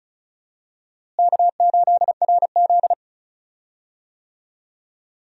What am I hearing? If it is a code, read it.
Morse code, heard as a keyed tone, about 35 wpm, 700 Hz. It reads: K8RZ
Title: USA Call Signs - 4 Characters - Repeated 2x - 35wpm